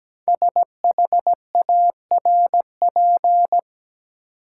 Morse code, heard as a keyed tone, about 17 wpm, 700 Hz. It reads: SHARP
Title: Sentences from Top 1600 Words - New Word Drill - 17wpm